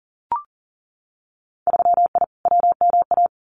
Morse code, heard as a keyed tone, about 41 wpm, 700 Hz. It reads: 3I WMA
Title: Winter Field Day Exchanges Encore 40wpm